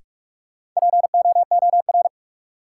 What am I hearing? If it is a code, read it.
Morse code, heard as a keyed tone, about 42 wpm, 700 Hz. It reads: POOR
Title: Sentences from Top 1000 Words - New Word Drill - 45wpm